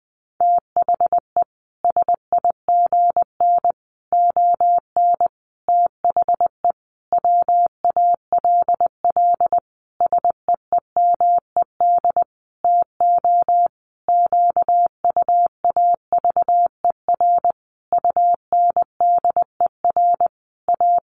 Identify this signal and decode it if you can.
Morse code, heard as a keyed tone, about 20 wpm, 700 Hz. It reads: THE SIGN ON THE WALL SEEMED TO QUAVER UNDER A
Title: A Sound Of Thunder 20wpm